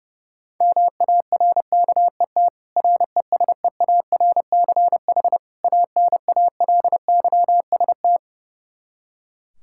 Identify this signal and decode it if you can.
Morse code, heard as a keyed tone, about 32 wpm, 700 Hz. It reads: MARKET RESEARCH ANALYST
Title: Practice QSO - Exchange 3 - More Info - 30wpm